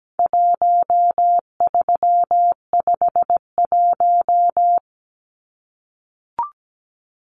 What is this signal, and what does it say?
Morse code, heard as a keyed tone, about 17 wpm, 700 Hz. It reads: 1351
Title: Numbers - 4-Digit 17wpm